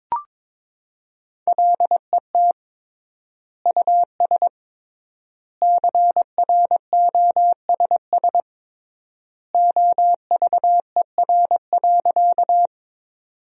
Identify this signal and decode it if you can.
Morse code, heard as a keyed tone, about 21 wpm, 700 Hz. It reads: LET US CROSS OVER.
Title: 3x Word-Spacing - Sentences from Top 300 Words - 22wpm